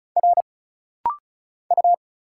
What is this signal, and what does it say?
Morse code, heard as a keyed tone, about 36 wpm, 700 Hz. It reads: R U
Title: Single Letter-Number Alt - Rapid-Fire 35wpm